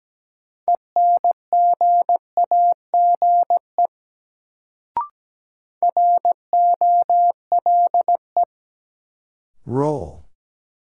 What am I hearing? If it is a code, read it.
Morse code, heard as a keyed tone, about 17 wpm, 700 Hz. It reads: ENGAGE ROLE
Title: Sentences from Top 1500 Words - New Word Drill Alt - 17wpm